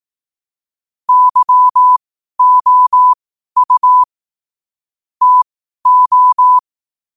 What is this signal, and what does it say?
Morse code, heard as a keyed tone, about 18 wpm, spaced as for 12 wpm, 1000 Hz. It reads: YOU TO
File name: qotdfiles / 12wpm